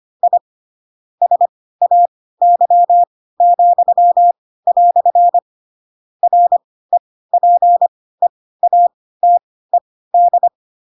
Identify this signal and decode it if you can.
Morse code, heard as a keyed tone, about 25 wpm, spaced as for 16 wpm, 700 Hz. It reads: I SAY," REPEATED
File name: Pinocchio / Pinocchio ch7 10 25 ik1hge